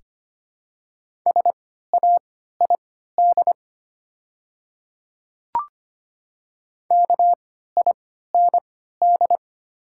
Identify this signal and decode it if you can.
Morse code, heard as a keyed tone, about 25 wpm, spaced as for 14 wpm, 700 Hz. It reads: SAID KIND
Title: 3x Character Spacing - Top 300 Words - 25wpm